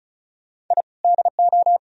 Morse code, heard as a keyed tone, about 46 wpm, spaced as for 40 wpm, 700 Hz. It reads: I DO
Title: Speed-Racing - Sets of 5 Words 35wpm